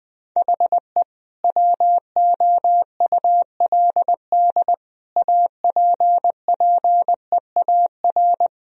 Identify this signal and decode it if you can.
Morse code, heard as a keyed tone, about 20 wpm, 700 Hz. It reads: HE WOULD APPEAR
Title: Sentences from Top 1000 Words - Review Part 2 - 20wpm